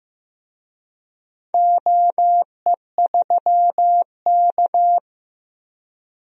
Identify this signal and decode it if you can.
Morse code, heard as a keyed tone, about 15 wpm, 700 Hz. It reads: OE3K
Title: DXCC-100 Call Signs - 4 Characters - Repeated 3x - 15wpm